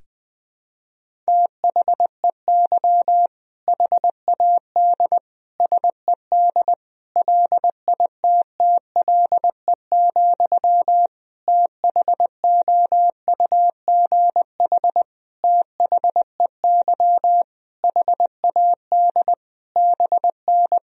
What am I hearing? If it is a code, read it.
Morse code, heard as a keyed tone, about 20 wpm, 700 Hz. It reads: THEY HAD SED LITTLE, THOUGH THEY HAD BN
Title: Sentences from Top 400 Words and Abbreviations 20wpm